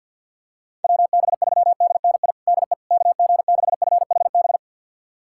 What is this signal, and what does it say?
Morse code, heard as a keyed tone, about 50 wpm, 700 Hz. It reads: WB3DNI DE KG6FHB
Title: USA Callsign DE Callsign - 6 Characters - 50wpm